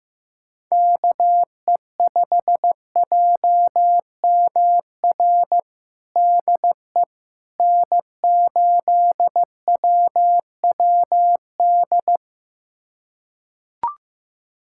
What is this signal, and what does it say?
Morse code, heard as a keyed tone, about 15 wpm, 700 Hz. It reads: KE5JMR DE N8WWD
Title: USA Callsign DE Callsign 15wpm